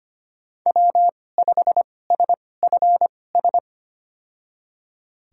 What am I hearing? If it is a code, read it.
Morse code, heard as a keyed tone, about 25 wpm, spaced as for 18 wpm, 700 Hz. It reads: W5SFS
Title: USA Call Signs - 5 Characters - 2x Character Spacing - 25wpm